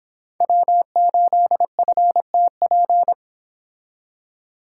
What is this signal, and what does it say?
Morse code, heard as a keyed tone, about 26 wpm, 700 Hz. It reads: W8FTP